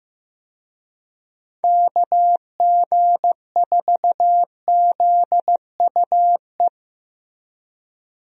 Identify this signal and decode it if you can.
Morse code, heard as a keyed tone, about 15 wpm, 700 Hz. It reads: KG4ZUE
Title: USA Call Signs - 6 Characters - Repeated 2x - 15wpm